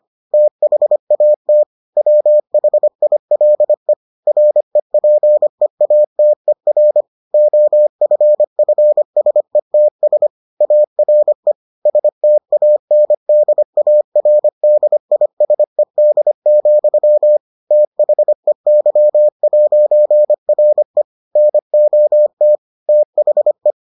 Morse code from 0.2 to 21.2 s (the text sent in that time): THAT WHILE REPEATER OFFSETS ARE STANDARDISED, THEY'RE